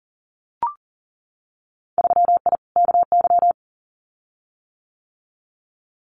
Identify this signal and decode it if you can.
Morse code, heard as a keyed tone, about 38 wpm, 700 Hz. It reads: 3I KY
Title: Winter Field Day Exchanges 40wpm